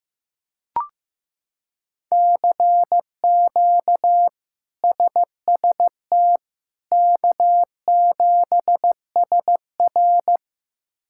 Morse code, heard as a keyed tone, about 15 wpm, 700 Hz. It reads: CQ SST K7SR
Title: K1USN SST QSO Practice - 15wpm